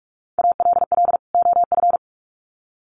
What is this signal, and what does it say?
Morse code, heard as a keyed tone, about 44 wpm, 700 Hz. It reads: ALL OF
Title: Sets of 2 Words 45wpm